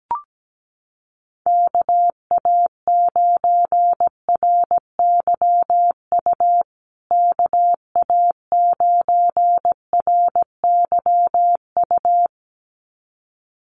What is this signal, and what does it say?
Morse code, heard as a keyed tone, about 17 wpm, 700 Hz. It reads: KA9RYU KA9RYU
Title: USA Call Signs Repeated 2x 17wpm